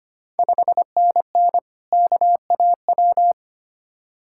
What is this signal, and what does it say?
Morse code, heard as a keyed tone, about 25 wpm, 700 Hz. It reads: 5NN KAW
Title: Ontario QSO Party Exchange 25wpm